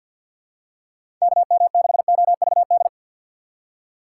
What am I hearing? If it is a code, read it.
Morse code, heard as a keyed tone, about 53 wpm, 700 Hz. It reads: KM6OVD
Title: USA Call Signs - 6x Character Spacing - 50wpm